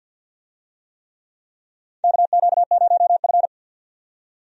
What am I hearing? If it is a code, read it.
Morse code, heard as a keyed tone, about 47 wpm, 700 Hz. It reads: KQ0F
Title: USA Call Signs - 4 Characters - 50wpm